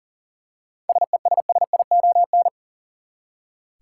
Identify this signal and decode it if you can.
Morse code, heard as a keyed tone, about 40 wpm, 700 Hz. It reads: SESSION
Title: Seven Letter Words - 40wpm